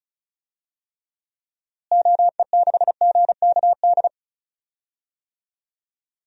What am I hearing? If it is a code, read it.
Morse code, heard as a keyed tone, about 34 wpm, 700 Hz. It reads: OE6GKD
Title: DXCC-100 Call Signs - 6 Characters - Repeated 2x - 35wpm